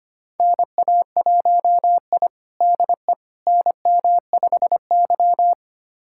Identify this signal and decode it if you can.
Morse code, heard as a keyed tone, about 25 wpm, 700 Hz. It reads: NA1I DE NM5Y
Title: USA Callsign DE Callsign - 4 Characters - 25wpm